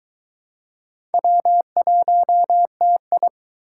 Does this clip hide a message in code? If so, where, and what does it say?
Yes: Morse code from 1.1 to 3.3 s, heard as a keyed tone, about 24 wpm, 700 Hz. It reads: W1TI